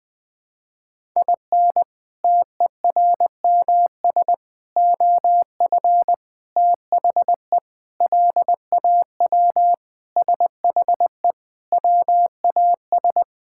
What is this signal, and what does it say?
Morse code, heard as a keyed tone, about 20 wpm, 700 Hz. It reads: IN TERMS OF THE LAW SHE WAS
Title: Sentences with 4-Word Phrases - Spoken Beginning - 20wpm